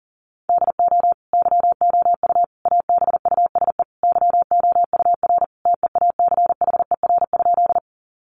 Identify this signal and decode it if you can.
Morse code, heard as a keyed tone, about 40 wpm, 700 Hz. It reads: DO YOU ABUSE YOUR TEACHER?
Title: Sentences from Top 1300 Words 40wpm